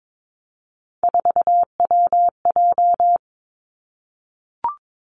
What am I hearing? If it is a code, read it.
Morse code, heard as a keyed tone, about 22 wpm, 700 Hz. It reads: 4WJ
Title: Three Challenging Letter-Number Combinations 22wpm